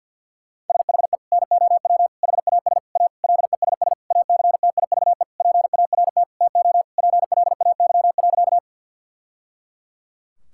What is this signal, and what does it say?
Morse code, heard as a keyed tone, about 49 wpm, 700 Hz. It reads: SHE NOW HAS A LESS ACTIVE PART TO PLAY.
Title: Sentences from Top 1000 Words - Review Part 6 - 50wpm